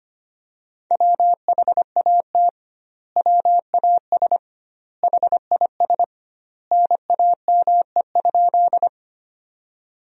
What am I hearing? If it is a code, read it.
Morse code, heard as a keyed tone, about 25 wpm, 700 Hz. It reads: WHAT WAS HIS NAME?
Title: 2x Word-Spacing - Sentences from Top 200 Words - 25wpm